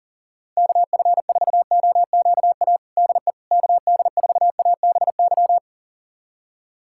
Morse code, heard as a keyed tone, about 40 wpm, 700 Hz. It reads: KF4OQA DE KD4ABY
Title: USA Callsign DE Callsign 40wpm